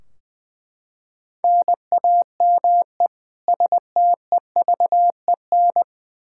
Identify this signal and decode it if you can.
Morse code, heard as a keyed tone, about 20 wpm, 700 Hz. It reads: NAME STEVEN
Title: Practice QSO - Exchange 1 - Beginning - 20wpm